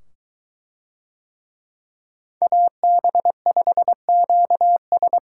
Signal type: Morse code

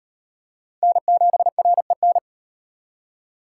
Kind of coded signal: Morse code

